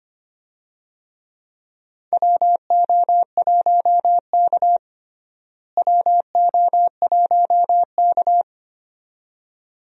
Morse code, heard as a keyed tone, about 25 wpm, 700 Hz. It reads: WO1K WO1K
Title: USA Call Signs - 4 Characters - Repeated 2x - 25wpm